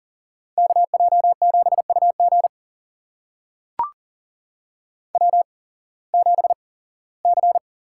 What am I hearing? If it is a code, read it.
Morse code, heard as a keyed tone, about 40 wpm, spaced as for 21 wpm, 700 Hz. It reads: KJ7UG W7C
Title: USA Call Signs - 5 Characters - 8x Character Spacing - 40wpm